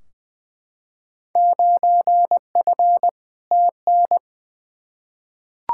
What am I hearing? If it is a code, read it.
Morse code, heard as a keyed tone, about 21 wpm, 700 Hz. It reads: 9F TN
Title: Field Day Exchanges Encore - 20wpm